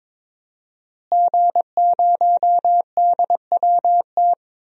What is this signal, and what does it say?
Morse code, heard as a keyed tone, about 22 wpm, 700 Hz. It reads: G0DWT
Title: DXCC-100 Call Signs - 5 Characters - Repeated 2x - 22wpm